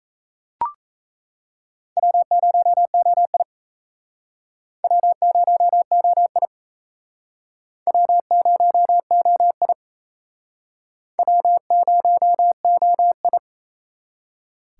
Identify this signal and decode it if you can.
Morse code, heard as a keyed tone, about 36 wpm, 700 Hz. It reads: W0OI W0OI W0OI W0OI